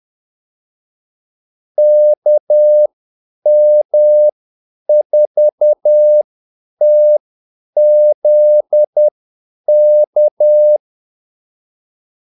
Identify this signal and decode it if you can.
Morse code, heard as a keyed tone, about 10 wpm, spaced as for 8 wpm, 600 Hz. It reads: KM4TZK
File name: Callsigns-10-8wpm